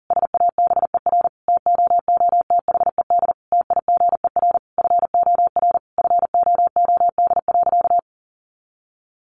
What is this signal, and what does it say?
Morse code, heard as a keyed tone, about 40 wpm, 700 Hz. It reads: SABER TOOTHED TIGER FOR FOOD.
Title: A Sound Of Thunder 40wpm